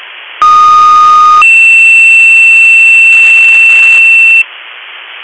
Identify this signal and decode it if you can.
Two-tone page, heard as a one-second tone, then a three-second tone, about 1230 then 2700 Hz. It is heard from 0.4 to 4.4 s.